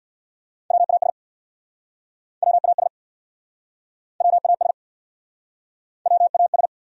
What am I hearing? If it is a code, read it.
Morse code, heard as a keyed tone, about 68 wpm, 700 Hz. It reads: WAS WAS WAS WAS